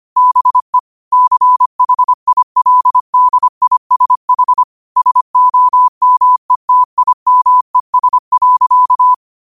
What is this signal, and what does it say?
Morse code, heard as a keyed tone, about 25 wpm, 1000 Hz. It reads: BE CHILDISH SOMETIMES.
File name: qotdfiles / 25wpm